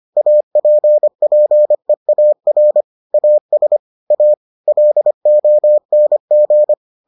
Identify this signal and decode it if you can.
Morse code, heard as a keyed tone, about 25 wpm, 600 Hz. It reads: APPEAR AS A LONG